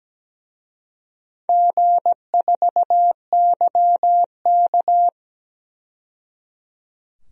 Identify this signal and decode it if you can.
Morse code, heard as a keyed tone, about 17 wpm, 700 Hz. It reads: G4YK
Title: DXCC-100 Call Signs - 4 Characters - 17wpm